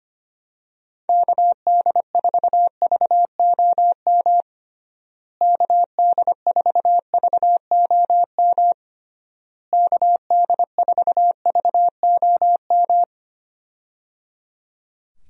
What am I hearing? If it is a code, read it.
Morse code, heard as a keyed tone, about 25 wpm, 700 Hz. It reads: KD4VOM KD4VOM KD4VOM
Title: USA Call Signs - 6 Characters - Repeated 3x - 25wpm